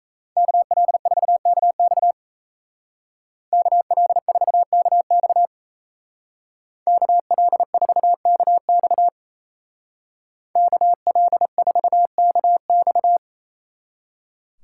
Morse code, heard as a keyed tone, about 34 wpm, 700 Hz. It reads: KL4KX KL4KX KL4KX KL4KX
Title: Speed Racing - USA Call Signs 28wpm